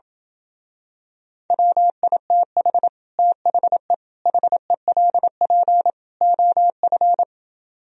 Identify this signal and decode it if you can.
Morse code, heard as a keyed tone, about 28 wpm, 700 Hz. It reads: WITH THE HELP OF